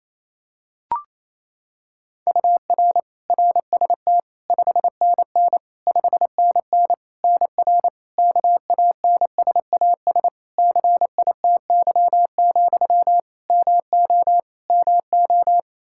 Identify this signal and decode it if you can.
Morse code, heard as a keyed tone, about 28 wpm, 700 Hz. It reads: UR RST 5NN 5NN NR KANSAS CITY, MO MO
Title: Practice QSO - Exchange 1 - Beginning - 28wpm